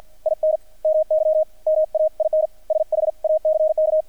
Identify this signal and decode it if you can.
Morse code, heard as a keyed tone, about 50 wpm, 630 Hz. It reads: IN MY MAIN SHACK
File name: archived / CRSnet